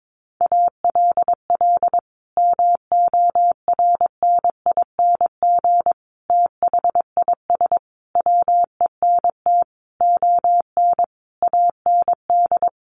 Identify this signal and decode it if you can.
Morse code, heard as a keyed tone, about 22 wpm, 700 Hz. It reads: ALL MORNING THIS WENT ON AND
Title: Sentences from Top 1500 Words 22wpm